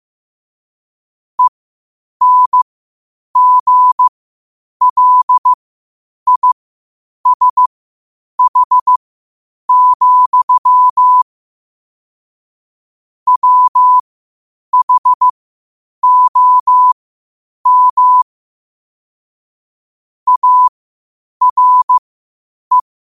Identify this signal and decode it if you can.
Morse code, heard as a keyed tone, about 15 wpm, spaced as for 8 wpm, 1000 Hz. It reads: ENGLISH, WHOM ARE
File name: qotdfiles / 8wpm